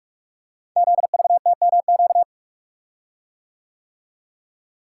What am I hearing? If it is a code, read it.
Morse code, heard as a keyed tone, about 44 wpm, 690 Hz. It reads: ZVTMQ
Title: Code Groups - 5 Letters - 45wpm